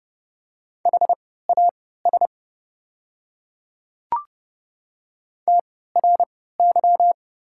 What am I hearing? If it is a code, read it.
Morse code, heard as a keyed tone, about 32 wpm, spaced as for 17 wpm, 700 Hz. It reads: HAS TRY
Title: 3x Character Spacing - Top 300 Words - 30wpm